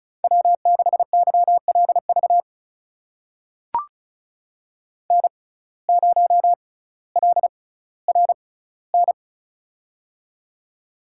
Morse code, heard as a keyed tone, about 36 wpm, spaced as for 18 wpm, 700 Hz. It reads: W6YLV N0LRN
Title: USA Call Signs - 5 Characters - 6x Character Spacing - 35wpm